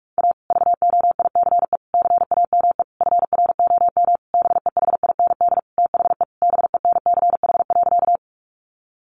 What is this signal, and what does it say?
Morse code, heard as a keyed tone, about 44 wpm, 700 Hz. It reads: A VOICE CAME FROM BEHIND THE BENCH.
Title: Sentences from Top 1400 Words 45wpm